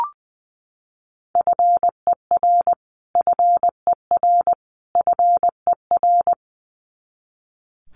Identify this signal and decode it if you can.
Morse code, heard as a keyed tone, about 20 wpm, 700 Hz. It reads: FER FER FER
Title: Abbreviations Repeated 3x - 20wpm